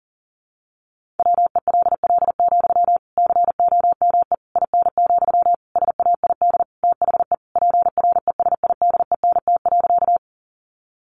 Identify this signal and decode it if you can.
Morse code, heard as a keyed tone, about 37 wpm, 700 Hz. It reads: WELL, COME IN, SAID THE PRESIDENT.
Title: Sentences from Top 1000 Words 40wpm